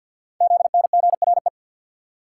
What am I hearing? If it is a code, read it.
Morse code, heard as a keyed tone, about 47 wpm, 700 Hz. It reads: ZNGRE
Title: Code Groups - 5 Letters - 50wpm